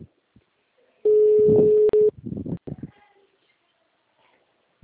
Ringback tone (Europe, 425 Hz).